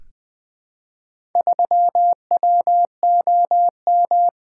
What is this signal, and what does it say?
Morse code, heard as a keyed tone, about 21 wpm, 700 Hz. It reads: 3WOM
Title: Four Letter-Number Combinations - 20wpm